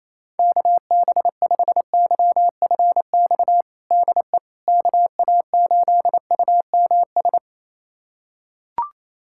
Morse code, heard as a keyed tone, about 27 wpm, 700 Hz. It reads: KB5YFX DE KA8UMS
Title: USA Callsign DE Callsign - 6 Characters - 28wpm